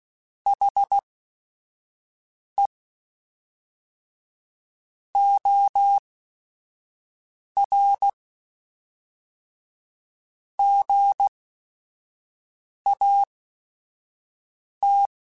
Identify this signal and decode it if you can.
Morse code, heard as a keyed tone, about 16 wpm, spaced as for 5 wpm, 800 Hz. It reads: HE OR GAT